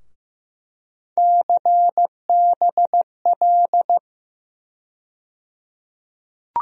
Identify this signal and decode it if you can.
Morse code, heard as a keyed tone, about 15 wpm, 700 Hz. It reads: CBL
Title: USA Call Sign Postfixes - 3 Letters 15wpm